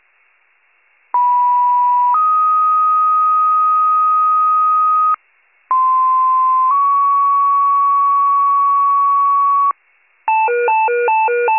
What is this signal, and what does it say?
Two-tone page, heard as a one-second tone, then a three-second tone, about 980 then 1250 Hz, about 1040 then 1120 Hz. It is heard from 1.1 to 9.7 s.